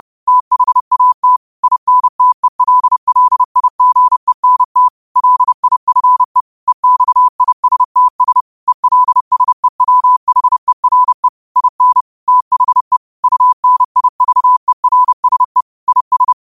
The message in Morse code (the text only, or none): THAT INTELLIGENT LIFE EXISTS ELSEWHERE IN THE UNIVERSE IS